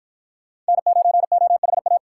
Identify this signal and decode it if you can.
Morse code, heard as a keyed tone, about 51 wpm, 700 Hz. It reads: N9OHA